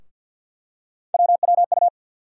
Morse code, heard as a keyed tone, about 51 wpm, 700 Hz. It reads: WWU
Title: USA Call Sign Postfixes - 3 Letters 50wpm